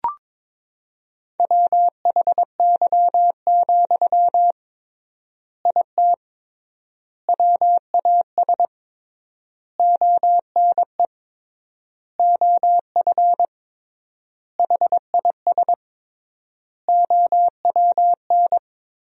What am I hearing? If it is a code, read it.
Morse code, heard as a keyed tone, about 22 wpm, 700 Hz. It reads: WHY, IT WAS ONE OF HIS OWN
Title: 3x Word-Spacing - Sentences from Top 300 Words - 22wpm